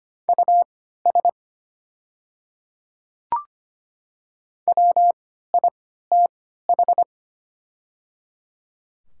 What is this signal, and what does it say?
Morse code, heard as a keyed tone, about 25 wpm, spaced as for 14 wpm, 700 Hz. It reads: US WITH
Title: 3x Character Spacing - Top 200 Words - 25wpm